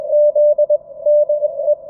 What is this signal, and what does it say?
Morse code, heard as a keyed tone, about 21 wpm, 590 Hz. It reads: Z DE
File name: F5NZY-QRP-5W-HS0ZME-20M-06-01-2019